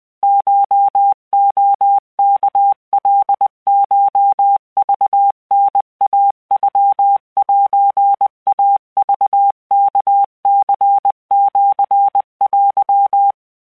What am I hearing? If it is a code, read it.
Morse code (Japanese Wabun code), heard as a keyed tone, about 20 wpm, 800 Hz. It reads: コレワカコクタイノセイクワニシテ
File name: keuikuchiyokugo20wptan